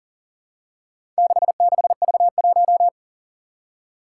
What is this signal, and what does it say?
Morse code, heard as a keyed tone, about 38 wpm, 700 Hz. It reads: 66V1
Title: Four Challenging Letter-Number Combinations 40wpm